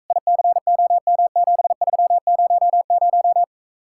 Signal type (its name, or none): Morse code